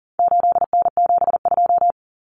Morse code, heard as a keyed tone, about 40 wpm, 700 Hz. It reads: 8N72
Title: Common Cut Numbers - 4-Digit 40wpm